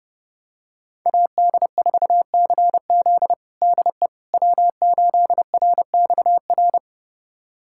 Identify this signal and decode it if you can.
Morse code, heard as a keyed tone, about 30 wpm, 700 Hz. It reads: AD4CZ DE W8RXR
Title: USA Callsign DE Callsign - 5 Characters - 30wpm